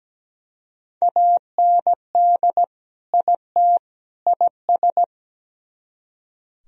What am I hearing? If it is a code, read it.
Morse code, heard as a keyed tone, about 20 wpm, 700 Hz. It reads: AND IT IS
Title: Speed-Racing - Sets of 3 Words 17wpm